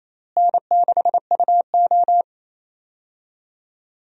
Morse code, heard as a keyed tone, about 28 wpm, 700 Hz. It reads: N6UO
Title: USA Call Signs - 4 Characters - Repeated 3x - 28wpm